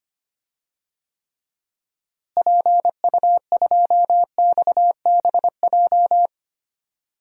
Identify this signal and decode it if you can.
Morse code, heard as a keyed tone, about 25 wpm, 700 Hz. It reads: PU2XBJ
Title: DXCC-100 Call Signs - 6 Characters - Repeated 2x - 25wpm